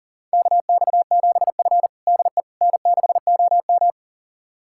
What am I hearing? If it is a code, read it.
Morse code, heard as a keyed tone, about 40 wpm, 700 Hz. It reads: KX7F DE N6OM
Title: USA Callsign DE Callsign - 4 Characters - 40wpm